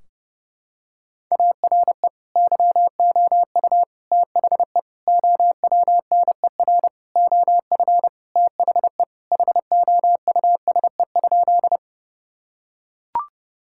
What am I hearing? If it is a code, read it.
Morse code, heard as a keyed tone, about 30 wpm, 700 Hz. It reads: ARE YOU THE OWNER OF THE HOUSE?
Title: Sentences from Top 1600 Words 30wpm